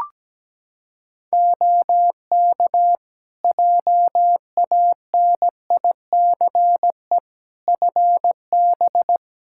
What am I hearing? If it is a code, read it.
Morse code, heard as a keyed tone, about 17 wpm, 700 Hz. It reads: OK JANICE FB
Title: K1USN SST QSO Practice 17wpm